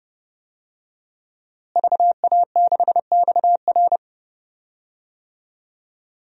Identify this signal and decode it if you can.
Morse code, heard as a keyed tone, about 30 wpm, 700 Hz. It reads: VA6XR
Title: DXCC-100 Call Signs - 5 Characters - Repeated 3x - 30wpm